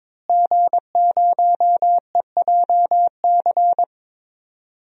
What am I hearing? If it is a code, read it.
Morse code, heard as a keyed tone, about 22 wpm, 700 Hz. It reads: G0EJC
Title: DXCC-100 Call Signs - 5 Characters - Repeated 3x - 22wpm